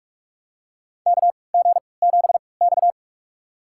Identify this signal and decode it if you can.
Morse code, heard as a keyed tone, about 44 wpm, spaced as for 25 wpm, 700 Hz. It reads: KG7X
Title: USA Call Signs - 4 Characters - 3x Character Spacing - 45wpm